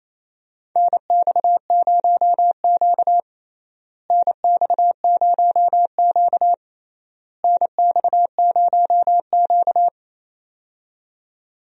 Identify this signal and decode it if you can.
Morse code, heard as a keyed tone, about 27 wpm, 700 Hz. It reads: NX0Q NX0Q NX0Q
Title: USA Call Signs - 4 Characters - Repeated 3x - 28wpm